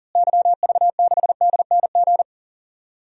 Morse code, heard as a keyed tone, about 40 wpm, 700 Hz. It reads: YV6DNG
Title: DXCC-100 Call Signs - 6 Characters - Repeated 2x - 40wpm